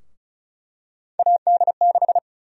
Morse code, heard as a keyed tone, about 36 wpm, 700 Hz. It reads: AD6
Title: USA Call Sign Prefixes - Repeated 2x - 35wpm